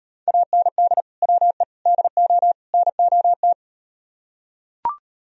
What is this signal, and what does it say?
Morse code, heard as a keyed tone, about 38 wpm, 700 Hz. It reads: AND WE DO NOT